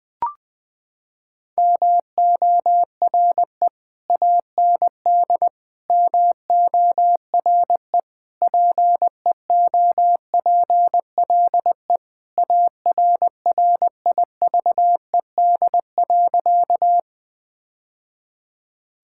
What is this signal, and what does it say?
Morse code, heard as a keyed tone, about 20 wpm, 700 Hz. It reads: MORE AND MORE PEOPLE ARRIVED.
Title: Sentences with 3-Word Phrases - 20wpm